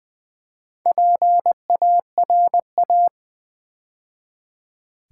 Morse code, heard as a keyed tone, about 20 wpm, 700 Hz. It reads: PARA
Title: Build A Word - 2 Letters At A Time - 6-Letter Words - 20wpm